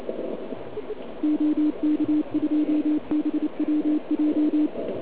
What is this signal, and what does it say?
Morse code, heard as a keyed tone, about 28 wpm, 300 Hz. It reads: OK2BWJ
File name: ARCHIV / KX3 / OK1IF CUC175